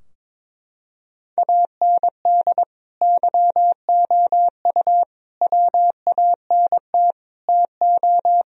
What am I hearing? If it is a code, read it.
Morse code, heard as a keyed tone, about 22 wpm, 700 Hz. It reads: AND YOU WANT TO
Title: Sentences from Top 1000 Words - Review Part 4 - 22wpm